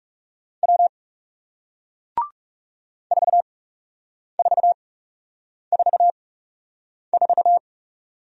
Morse code, heard as a keyed tone, about 41 wpm, spaced as for 8 wpm, 700 Hz. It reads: W 4444